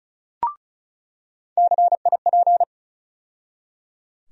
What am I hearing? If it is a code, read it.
Morse code, heard as a keyed tone, about 36 wpm, 700 Hz. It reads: CIP
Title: USA Call Sign Postfixes - 3 Letters 35wpm